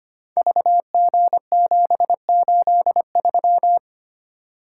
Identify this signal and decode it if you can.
Morse code, heard as a keyed tone, about 25 wpm, 700 Hz. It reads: VG783
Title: Code Groups - 5 Letter:Numbers - 25wpm